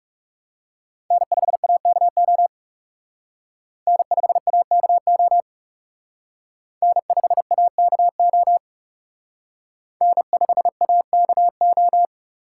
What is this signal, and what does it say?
Morse code, heard as a keyed tone, about 38 wpm, 700 Hz. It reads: N5AKO N5AKO N5AKO N5AKO